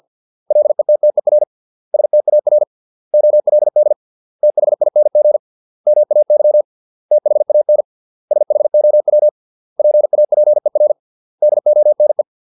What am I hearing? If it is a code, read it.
Morse code, heard as a keyed tone, about 50 wpm, 600 Hz. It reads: LETTER STAR OLD THING MAY THAN SHOW PAPER DONE